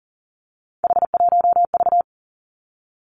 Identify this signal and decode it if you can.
Morse code, heard as a keyed tone, about 40 wpm, 700 Hz. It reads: H1V
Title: Three Challenging Letter-Number Combinations 40wpm